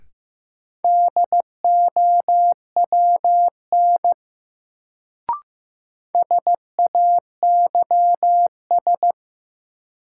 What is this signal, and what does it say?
Morse code, heard as a keyed tone, about 16 wpm, 700 Hz. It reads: DOWN SAYS
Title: Sets of 2 Words - New Word Drill - 15wpm